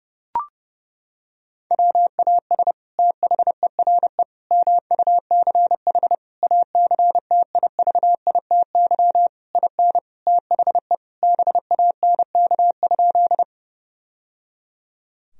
Morse code, heard as a keyed tone, about 32 wpm, 700 Hz. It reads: WAS THERE MUCH ACTIVITY IN THE BANK?
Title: Sentences from Top 1000 Words - Review Part 4 - 30wpm